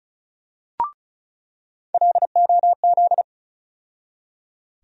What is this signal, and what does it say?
Morse code, heard as a keyed tone, about 36 wpm, 700 Hz. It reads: LOZ
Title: USA Call Sign Postfixes - 3 Letters 35wpm